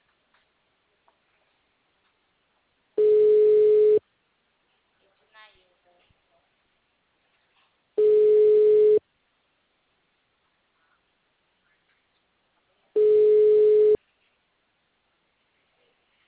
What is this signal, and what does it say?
Ringback tone (Europe, 425 Hz), with speech.